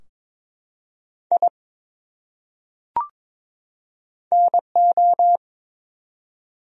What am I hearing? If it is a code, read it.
Morse code, heard as a keyed tone, about 22 wpm, 700 Hz. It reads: I NO
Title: Top 100 Words 22wpm